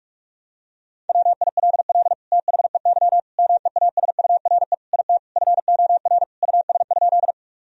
Morse code, heard as a keyed tone, about 46 wpm, 700 Hz. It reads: WILL THEY MEASURE IT FOR US?
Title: Sentences from Top 1000 Words - Review Part 2 - 45wpm